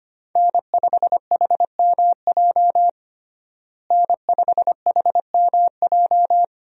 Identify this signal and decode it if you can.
Morse code, heard as a keyed tone, about 25 wpm, 700 Hz. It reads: N5HMJ N5HMJ
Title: USA Call Signs - 5 Characters - Repeated 3x - 25wpm